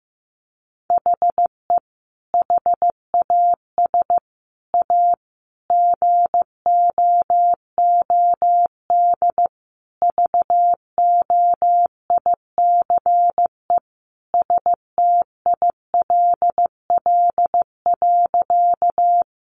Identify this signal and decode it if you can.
Morse code, heard as a keyed tone, about 14 wpm, 700 Hz. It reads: HE HAS A GOOD VOICE STILL.
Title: Sentences from Top 500 Words 15wpm